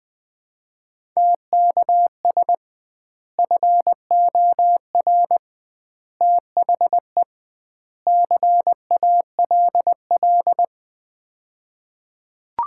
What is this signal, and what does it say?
Morse code, heard as a keyed tone, about 21 wpm, 700 Hz. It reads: TKS FOR THE CALL
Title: Common QSO Phrases - 2x Word-Spacing - 20wpm